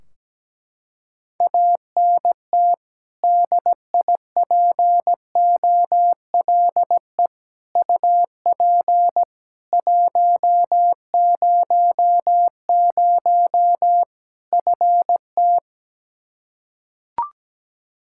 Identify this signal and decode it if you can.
Morse code, heard as a keyed tone, about 17 wpm, 700 Hz. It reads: ANT DIPOLE UP 100 FT
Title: Practice QSO - Exchange 2 - Info - 17wpm